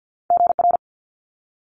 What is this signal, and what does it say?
Morse code, heard as a keyed tone, about 50 wpm, 700 Hz. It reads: GR
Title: Multi-letter Phonemes Quick Fire 50wpm